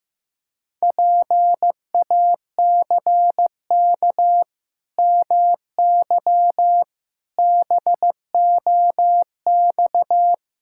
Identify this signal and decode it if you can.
Morse code, heard as a keyed tone, about 15 wpm, 700 Hz. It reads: PACK MY BOX
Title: Pangram Challenge v2 - 15wpm